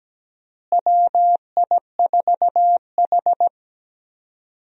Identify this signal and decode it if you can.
Morse code, heard as a keyed tone, about 17 wpm, 700 Hz. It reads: WI4H
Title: USA Call Signs - 4 Characters - Repeated 2x - 17wpm